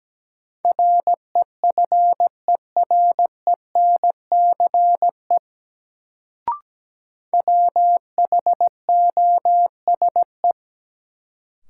Morse code, heard as a keyed tone, about 17 wpm, 700 Hz. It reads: REFERENCE WHOSE
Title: Sentences from Top 1200 Words - New Word Drill Alt - 17wpm